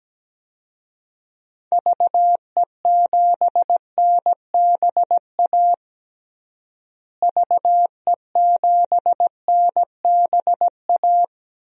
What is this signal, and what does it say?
Morse code, heard as a keyed tone, about 17 wpm, 700 Hz. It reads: VE7NBA VE7NBA
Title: DXCC-100 Call Signs - 6 Characters - Repeated 2x - 17wpm